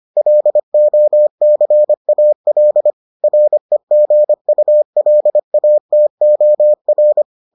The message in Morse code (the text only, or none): LOCAL REGULATOR